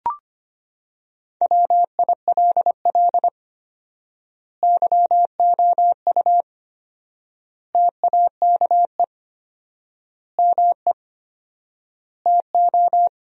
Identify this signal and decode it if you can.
Morse code, heard as a keyed tone, about 25 wpm, 700 Hz. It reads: WILL YOU TAKE ME TO
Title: 4x Word-Spacing - Sentences from Top 200 Words - 25wpm